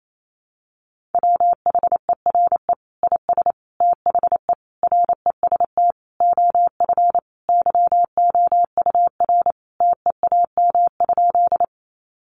Morse code, heard as a keyed tone, about 28 wpm, 700 Hz. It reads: WHERE IS THE REST OF YOUR TEAM?
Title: Sentences from Top 1500 Words 28wpm